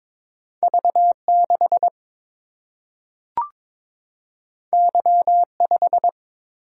Morse code, heard as a keyed tone, about 22 wpm, 700 Hz. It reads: V 6 Y 5
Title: Single-Letter Single-Number 22wpm